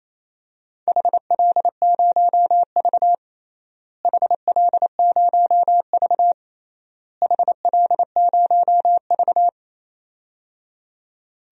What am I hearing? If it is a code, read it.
Morse code, heard as a keyed tone, about 28 wpm, 700 Hz. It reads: HL0V HL0V HL0V
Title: DXCC-100 Call Signs - 4 Characters - Repeated 3x - 28wpm